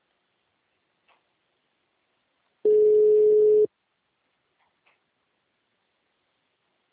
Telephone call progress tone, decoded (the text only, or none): ringback tone